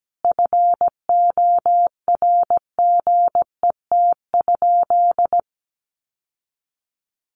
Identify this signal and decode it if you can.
Morse code, heard as a keyed tone, about 17 wpm, 700 Hz. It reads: FORGET?
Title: Sentences from Top 1300 Words 17wpm